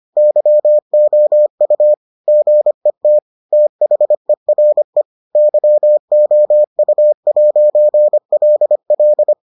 Morse code, heard as a keyed tone, about 25 wpm, 600 Hz. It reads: YOU GET THERE YOU'LL